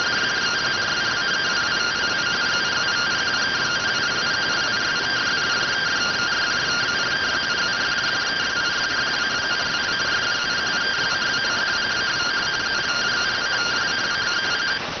Two-tone teleprinter signal, about 170 Hz shift.